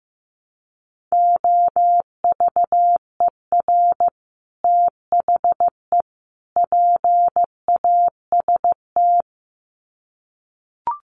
Morse code, heard as a keyed tone, about 15 wpm, 700 Hz. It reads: OVER THE PAST
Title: Sets of 3 Words 15wpm